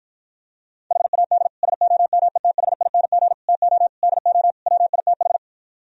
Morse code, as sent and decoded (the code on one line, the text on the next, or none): .... .- -.. / ... --- -- . - .... .. -. --. / - --- / -.. --- / .-- .. - ....
HAD SOMETHING TO DO WITH